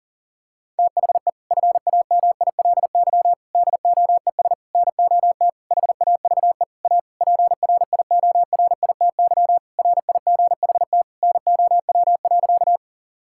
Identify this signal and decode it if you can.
Morse code, heard as a keyed tone, about 40 wpm, 700 Hz. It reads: THE FAMILY DOES NOT HAVE A PRIORITY RIGHT NOW.
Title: Sentences from Top 1600 Words 40wpm